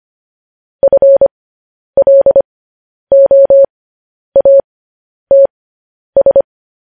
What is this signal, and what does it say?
Morse code, heard as a keyed tone, about 25 wpm, spaced as for 10 wpm, 550 Hz. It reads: FLOATS
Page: news-headlines-in-morse-code-at-10-wpm